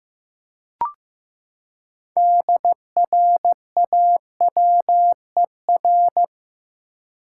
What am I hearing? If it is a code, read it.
Morse code, heard as a keyed tone, about 15 wpm, 700 Hz. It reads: DRAWER
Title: Six Letter Words - 15wpm